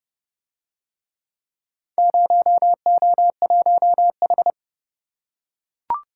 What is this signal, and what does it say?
Morse code, heard as a keyed tone, about 28 wpm, 700 Hz. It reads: 0O1H